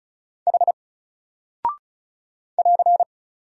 Morse code, heard as a keyed tone, about 36 wpm, 700 Hz. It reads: H <AR>
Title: Single Character (Letter:Number:Punctuation:Prosign) 35wpm